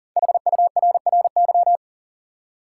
Morse code, heard as a keyed tone, about 38 wpm, 700 Hz. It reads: HURRY
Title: Five Letter Words - 40wpm